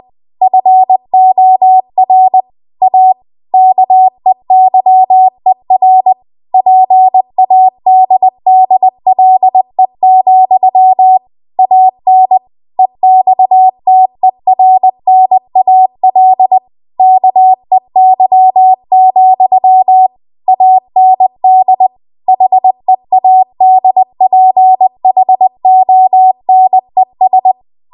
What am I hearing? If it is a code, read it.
Morse code, heard as a keyed tone, about 20 wpm, 750 Hz. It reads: FOR A KEYER PADDLE, AN EXTERNAL KEY, AND HEADPHONES